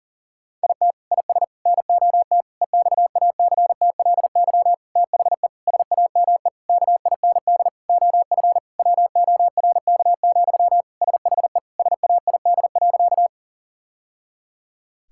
Morse code, heard as a keyed tone, about 40 wpm, 700 Hz. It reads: IT IS NOT EXACTLY THE SAME KIND OF WORK, SHE SAID.
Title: Sentences from Top 1000 Words - Review Part 6 - 40wpm